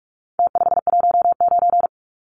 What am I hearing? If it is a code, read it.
Morse code, heard as a keyed tone, about 44 wpm, 700 Hz. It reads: T519
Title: Numbers - Contest Style - 4-Digit 45wpm